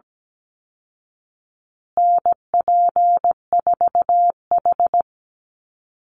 Morse code, heard as a keyed tone, about 17 wpm, 700 Hz. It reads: NP4H
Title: Call Signs Repeated 3x - 17wpm